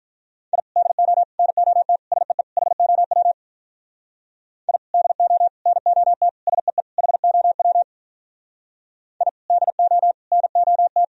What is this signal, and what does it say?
Morse code, heard as a keyed tone, about 43 wpm, 700 Hz. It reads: I DO NOT SEE HOW I DO NOT SEE HOW I DO NOT
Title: Speed-Racing - Sets of 5 Words 35wpm